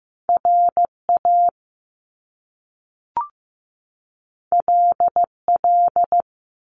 Morse code, heard as a keyed tone, about 15 wpm, 700 Hz. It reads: RA LL
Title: Top Two Letter Combinations in Words 15wpm